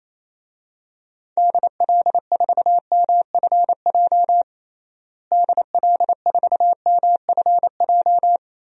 Morse code, heard as a keyed tone, about 28 wpm, 700 Hz. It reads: DL4MFJ DL4MFJ
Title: DXCC-100 Call Signs - 6 Characters - Repeated 3x - 28wpm